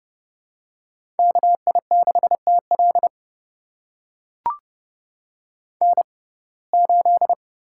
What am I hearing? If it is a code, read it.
Morse code, heard as a keyed tone, about 30 wpm, spaced as for 22 wpm, 700 Hz. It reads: KI6TL N8
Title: USA Call Signs - 5 Characters - 6x Character Spacing - 30wpm